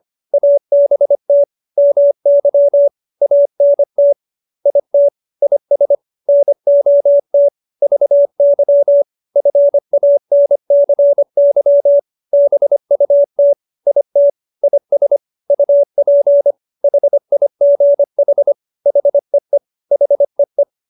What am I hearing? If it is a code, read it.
Morse code, heard as a keyed tone, about 25 wpm, 570 Hz. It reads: ABT MY ANT IT IS NOT VY FANCY BUT IT IS UP HIGH HEE HEE